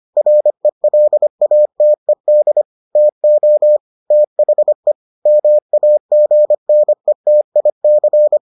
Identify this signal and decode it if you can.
Morse code, heard as a keyed tone, about 25 wpm, 600 Hz. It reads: RELATED TO THE MAGNETIC